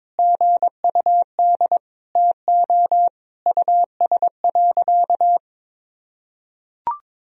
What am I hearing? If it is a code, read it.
Morse code, heard as a keyed tone, about 22 wpm, 700 Hz. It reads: GUD TO US.
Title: Sentences from Top 200 Words and Abbreviations 22wpm